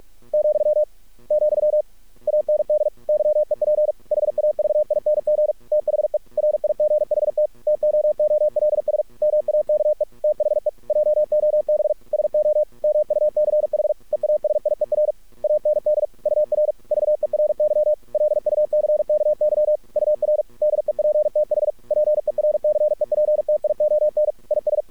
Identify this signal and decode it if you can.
Morse code, heard as a keyed tone, about 46 wpm, 600 Hz. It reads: , , AND YEP HAVING THE RIGHT TOOLS MAKE THE JOB SO MUCH EASIER AND UR VERY LUCKY UR DEPTH PERCEPTION IS
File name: archived / CRSnet